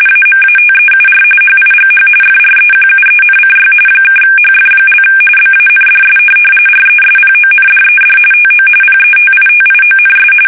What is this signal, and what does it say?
RTTY, about 45.45 baud. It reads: AMATEUR HIGH FREQUENCY (HF) BANDS RESPONDED TO PETITIONS
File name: fsk850 / teletype-RTTY